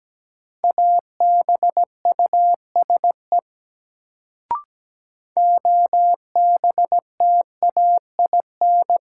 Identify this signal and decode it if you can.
Morse code, heard as a keyed tone, about 17 wpm, 700 Hz. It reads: ABUSE OBTAIN
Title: Sentences from Top 1300 Words - New Word Drill Alt - 17wpm